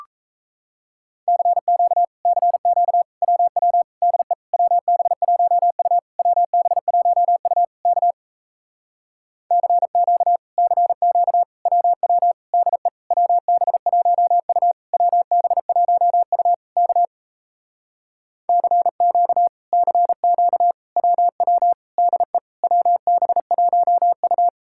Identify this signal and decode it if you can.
Morse code, heard as a keyed tone, about 38 wpm, 700 Hz. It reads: CQ CQ WW DE WB1U WB1U K CQ CQ WW DE WB1U WB1U K CQ CQ WW DE WB1U